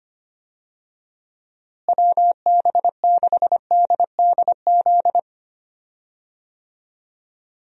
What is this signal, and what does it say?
Morse code, heard as a keyed tone, about 25 wpm, 700 Hz. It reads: WB6DDZ
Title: USA Call Signs - 6 Characters - Repeated 3x - 25wpm